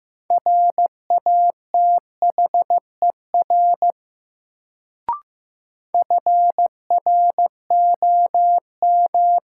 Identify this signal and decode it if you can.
Morse code, heard as a keyed tone, about 15 wpm, 700 Hz. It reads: RATHER FROM
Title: Sets of 2 Words - New Word Drill - 15wpm